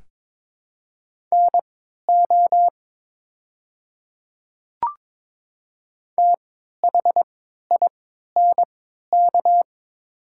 Morse code, heard as a keyed tone, about 22 wpm, spaced as for 12 wpm, 700 Hz. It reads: NO THINK
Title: 3x Character Spacing - Top 200 Words - 22wpm